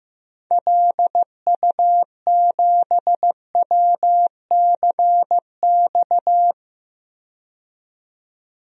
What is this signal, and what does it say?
Morse code, heard as a keyed tone, about 15 wpm, 700 Hz. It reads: LU7WCX
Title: DXCC-100 Call Signs - 6 Characters - 15wpm